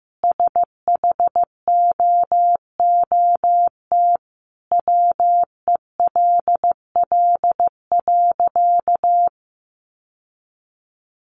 Morse code, heard as a keyed tone, about 15 wpm, 700 Hz. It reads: SHOOT WELL.
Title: Sentences from Top 1200 Words 15wpm